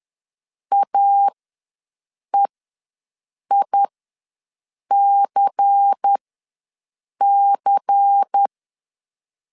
Morse code, heard as a keyed tone, about 10 wpm, spaced as for 6 wpm, 800 Hz. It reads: AEICC
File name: LEZIONI / 5 LBVC / 5 LBVC ws 10wpm 3733